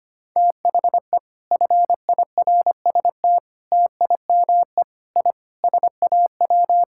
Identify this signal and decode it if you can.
Morse code, heard as a keyed tone, about 25 wpm, 700 Hz. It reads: THE FIRST TIME I SAW
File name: practice / 25wpm / Sentences with 4-Word Phrases - 25wpm